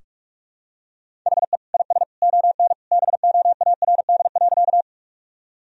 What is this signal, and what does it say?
Morse code, heard as a keyed tone, about 46 wpm, 700 Hz. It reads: HE IS ON BOARD.
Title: Sentences from Top 1000 Words - Review Part 4 - 45wpm